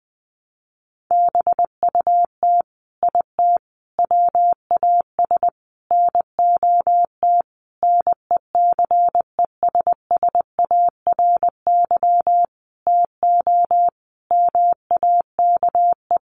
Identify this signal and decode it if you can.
Morse code, heard as a keyed tone, about 20 wpm, 700 Hz. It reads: BUT IT WAS NOT NECESSARY TO MAKE
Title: Sentences from Top 1200 Words 20wpm